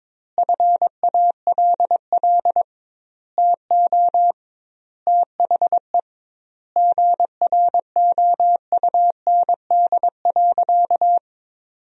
Morse code, heard as a keyed tone, about 22 wpm, 700 Hz. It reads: FALL TO THE GROUND.
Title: 2x Word-Spacing - Sentences from Top 500 Words - 22wpm